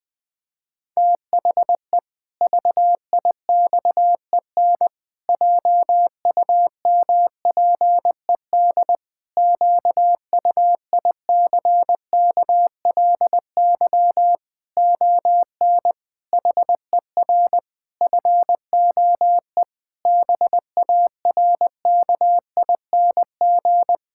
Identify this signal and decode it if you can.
Morse code, heard as a keyed tone, about 19 wpm, 700 Hz. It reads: THE VIXEN JUMPED QUICKLY ON HER FOE BARKING
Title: Pangram Challenge v2 - 20wpm